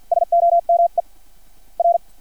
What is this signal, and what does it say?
Morse code, heard as a keyed tone, about 50 wpm, spaced as for 44 wpm, 690 Hz. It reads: SOME R
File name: archived / CRSnet